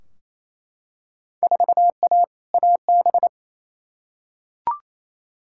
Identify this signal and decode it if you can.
Morse code, heard as a keyed tone, about 28 wpm, 700 Hz. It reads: 4A AB
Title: Field Day Exchanges Encore - 28wpm